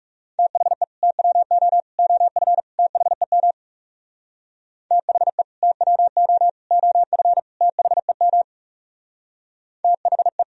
Morse code, heard as a keyed tone, about 36 wpm, 700 Hz. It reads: THE TWO OF THEM THE TWO OF THEM THE
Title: Speed-Racing - Sets of 4 Words 30wpm